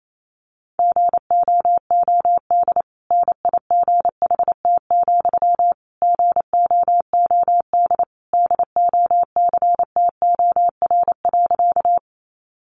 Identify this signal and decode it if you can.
Morse code, heard as a keyed tone, about 28 wpm, 700 Hz. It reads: GOOD NIGHT, GOOD DOCTOR.
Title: Sentences from Top 1500 Words 28wpm